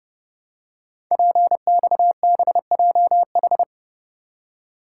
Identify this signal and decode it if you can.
Morse code, heard as a keyed tone, about 30 wpm, 700 Hz. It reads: PXBJH
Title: Code Groups - 5 Letters - 30wpm